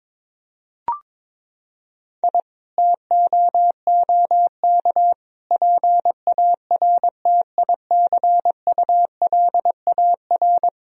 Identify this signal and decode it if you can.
Morse code, heard as a keyed tone, about 22 wpm, 700 Hz. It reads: I TOOK PARTICULAR
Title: Sentences from Top 1000 Words - Review Part 4 - 22wpm